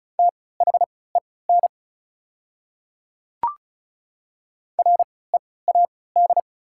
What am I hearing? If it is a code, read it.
Morse code, heard as a keyed tone, about 36 wpm, spaced as for 20 wpm, 690 Hz. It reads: THEN READ
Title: 3x Character Spacing - Top 200 Words - 35wpm